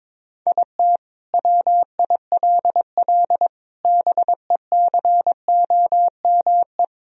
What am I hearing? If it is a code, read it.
Morse code, heard as a keyed tone, about 22 wpm, 700 Hz. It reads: IT WILL BECOME
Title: Sentences from Top 1000 Words - Review Part 3 - 22wpm